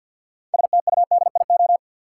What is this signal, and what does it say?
Morse code, heard as a keyed tone, about 53 wpm, 700 Hz. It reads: STUDIO
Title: Sentences from Top 1200 Words - New Word Drill Alt - 50wpm